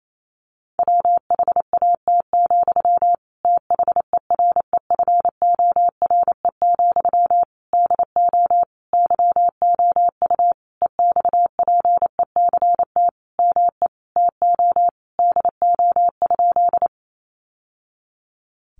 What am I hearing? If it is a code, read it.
Morse code, heard as a keyed tone, about 28 wpm, 700 Hz. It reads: WHAT, THEREFORE, DO YOU EXPECT ME TO DO?
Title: Sentences from Top 1100 Words 28wpm